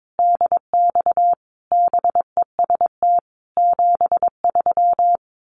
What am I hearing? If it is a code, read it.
Morse code, heard as a keyed tone, about 22 wpm, 700 Hz. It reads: DX BEST 73
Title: Common QSO Phrases 22wpm